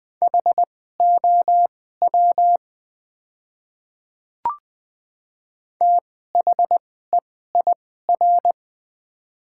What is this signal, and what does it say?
Morse code, heard as a keyed tone, about 20 wpm, spaced as for 14 wpm, 700 Hz. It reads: HOW THEIR
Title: 2x Character Spacing - Top 100 Words - 20wpm